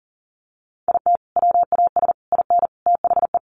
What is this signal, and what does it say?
Morse code, heard as a keyed tone, about 40 wpm, 700 Hz. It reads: IT WAS IN THE
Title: Sentences from Top 1300 Words 40wpm